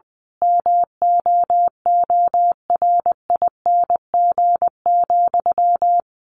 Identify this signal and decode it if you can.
Morse code, heard as a keyed tone, about 20 wpm, 700 Hz. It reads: MOORING,
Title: A Sound Of Thunder 20wpm